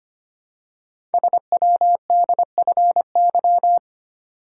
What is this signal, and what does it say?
Morse code, heard as a keyed tone, about 25 wpm, 700 Hz. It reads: SWDFY
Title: Code Groups - 5 Letters - 25wpm